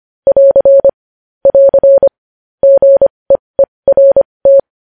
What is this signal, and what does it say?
Morse code, heard as a keyed tone, about 25 wpm, spaced as for 20 wpm, 550 Hz. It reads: <AR> <AR> GEERT
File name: podcast / News headlines-1x-25x20